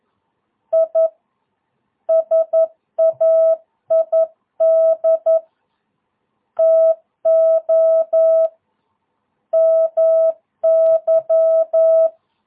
Morse code, heard as a keyed tone, about 11 wpm, 650 Hz. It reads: I SAID TO MY
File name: jh8cbh / 2022cw